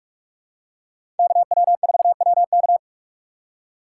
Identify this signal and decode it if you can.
Morse code, heard as a keyed tone, about 43 wpm, 700 Hz. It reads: KW4WK
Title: USA Call Signs - 5 Characters - 3x Character Spacing - 45wpm